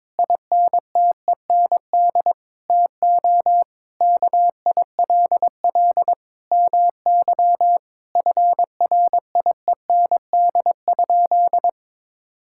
Morse code, heard as a keyed tone, about 22 wpm, 700 Hz. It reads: INTEND TO KILL MY FRIEND?
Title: Sentences from Top 1600 Words 22wpm